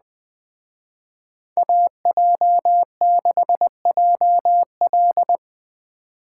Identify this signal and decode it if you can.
Morse code, heard as a keyed tone, about 20 wpm, 700 Hz. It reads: AJ6JL